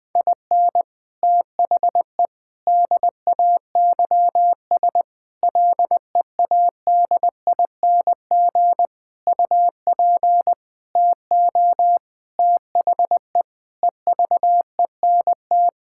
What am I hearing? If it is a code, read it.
Morse code, heard as a keyed tone, about 20 wpm, 700 Hz. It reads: IN THE DAYS LEADING UP TO THE EVENT
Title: Sentences with 7-Word Phrases - Spoken Beginning - 20wpm